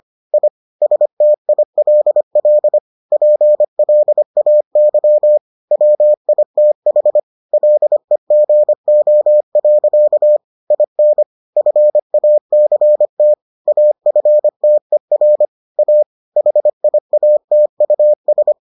Morse code, heard as a keyed tone, about 25 wpm, 600 Hz. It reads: I STILL PLAY WITH LEGO. IN FACT AFTER A HIATUS